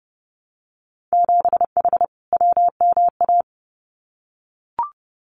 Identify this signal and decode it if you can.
Morse code, heard as a keyed tone, about 30 wpm, 700 Hz. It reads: 7H WMA
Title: Winter Field Day Exchanges Encore 30wpm